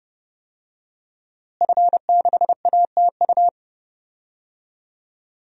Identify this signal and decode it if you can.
Morse code, heard as a keyed tone, about 32 wpm, 700 Hz. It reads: F6ATU
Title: DXCC-100 Call Signs - 5 Characters - Repeated 3x - 30wpm